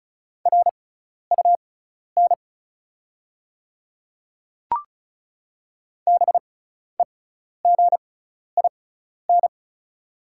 Morse code, heard as a keyed tone, about 36 wpm, spaced as for 12 wpm, 700 Hz. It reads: RUN BEGIN
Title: 6x Character Spacing - Top 300 Words - 35wpm